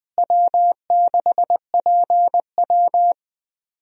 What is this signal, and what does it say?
Morse code, heard as a keyed tone, about 21 wpm, 700 Hz. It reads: W6PW
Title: USA Call Signs - 4 Characters - Repeated 2x - 20wpm